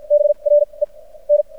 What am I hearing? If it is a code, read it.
Morse code, heard as a keyed tone, about 56 wpm, spaced as for 26 wpm, 590 Hz. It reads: BREN
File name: archived / CRSnet